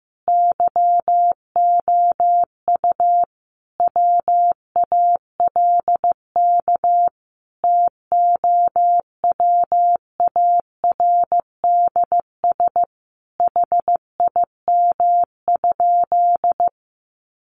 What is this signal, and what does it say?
Morse code, heard as a keyed tone, about 15 wpm, 700 Hz. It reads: YOU WALK TOWARDS HIM?
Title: Sentences from Top 1100 Words 15wpm